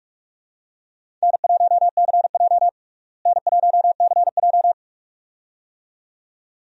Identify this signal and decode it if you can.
Morse code, heard as a keyed tone, about 48 wpm, 700 Hz. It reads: N1CJ N1CJ
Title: USA Call Signs - 4 Characters - Repeated 2x - 45wpm